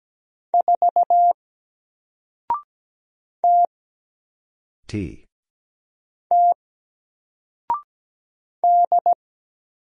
Morse code, heard as a keyed tone, about 17 wpm, 700 Hz. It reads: 4 T T D
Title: Single Character (Letter:Number:Punctuation:Prosign) 17wpm